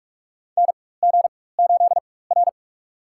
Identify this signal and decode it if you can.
Morse code, heard as a keyed tone, about 42 wpm, spaced as for 20 wpm, 700 Hz. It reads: NG8R
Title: USA Call Signs - 4x Character Spacing - 45wpm